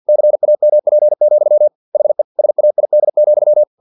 Morse code, heard as a keyed tone, about 50 wpm, 600 Hz. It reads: CAMP, HE SAID,
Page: Never 49wpm